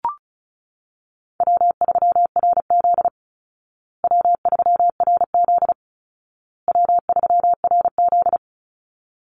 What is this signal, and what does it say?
Morse code, heard as a keyed tone, about 34 wpm, 700 Hz. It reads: W3RZ W3RZ W3RZ
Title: Easy Call Signs Repeated 3x - 35wpm